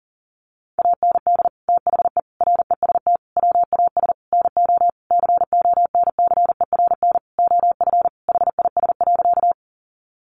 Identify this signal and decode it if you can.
Morse code, heard as a keyed tone, about 40 wpm, 700 Hz. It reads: AND THE REST WAS NO CONCERN OF HIS.
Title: Sentences from Top 1000 Words 40wpm